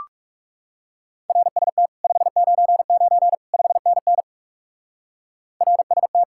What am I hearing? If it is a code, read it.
Morse code, heard as a keyed tone, about 37 wpm, 700 Hz. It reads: RST 599 5NN RST
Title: Speed-Racing Common QSO Phrases Encore - 30wpm